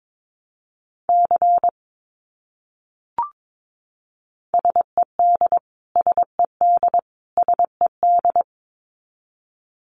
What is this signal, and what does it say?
Morse code, heard as a keyed tone, about 22 wpm, 700 Hz. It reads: C SED SED SED
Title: Abbreviations Repeated 3x - 22wpm